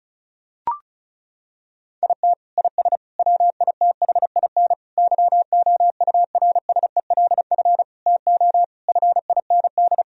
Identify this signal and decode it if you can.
Morse code, heard as a keyed tone, about 36 wpm, 700 Hz. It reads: IT IS WITHIN YOURSELF TO FIND
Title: Sentences from Top 1000 Words - Review Part 4 - 35wpm